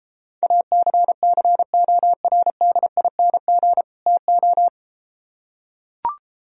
Morse code, heard as a keyed tone, about 33 wpm, 700 Hz. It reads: ACCORDING TO